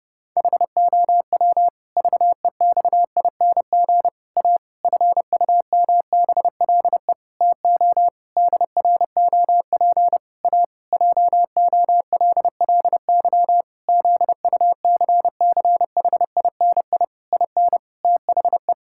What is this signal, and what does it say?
Morse code, heard as a keyed tone, about 30 wpm, 700 Hz. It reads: HOW VEXING A FUMBLE TO DROP A JOLLY ZUCCHINI IN THE
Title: Pangram Challenge v2 - 30wpm